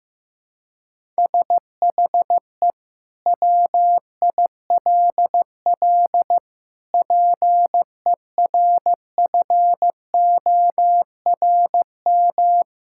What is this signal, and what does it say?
Morse code, heard as a keyed tone, about 15 wpm, 700 Hz. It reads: SHE WILL PERFORM
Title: Sentences from Top 1000 Words - Review Part 5 - 15wpm